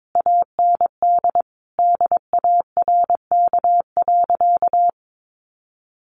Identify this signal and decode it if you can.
Morse code, heard as a keyed tone, about 22 wpm, 700 Hz. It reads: AND DARK.
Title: Sentences from Top 600 Words 22wpm